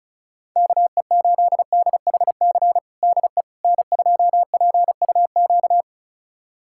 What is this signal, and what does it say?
Morse code, heard as a keyed tone, about 34 wpm, 700 Hz. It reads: KE8DHC DE N2PUQ
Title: USA Callsign DE Callsign 35wpm